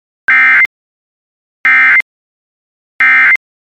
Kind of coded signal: Emergency Alert System (SAME) tones